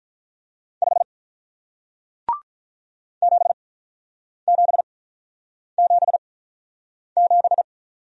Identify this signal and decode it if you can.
Morse code, heard as a keyed tone, about 47 wpm, spaced as for 9 wpm, 700 Hz. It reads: 5 7777